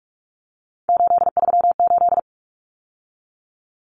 Morse code, heard as a keyed tone, about 44 wpm, 700 Hz. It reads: 838
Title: Numbers - 3-Digit 45wpm